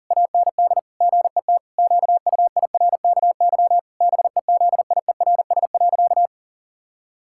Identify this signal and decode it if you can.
Morse code, heard as a keyed tone, about 40 wpm, 700 Hz. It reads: AND GET QUIRKY BEZIERS.
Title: Pangram Challenge v2 - 40wpm